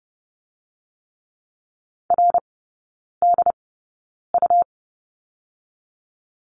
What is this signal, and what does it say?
Morse code, heard as a keyed tone, about 30 wpm, spaced as for 9 wpm, 700 Hz. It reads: RDU
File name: Lesson 17e - Letter u - Three Characters 30wpm